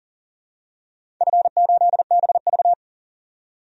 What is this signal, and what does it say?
Morse code, heard as a keyed tone, about 38 wpm, 700 Hz. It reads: F8BV
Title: DXCC-100 Call Signs - 4 Characters - 40wpm